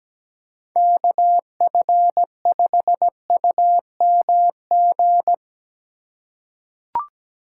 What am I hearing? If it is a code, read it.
Morse code, heard as a keyed tone, about 17 wpm, 700 Hz. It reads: KF5UMG
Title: USA Call Signs - 6 Characters - 2x Character Spacing - 17wpm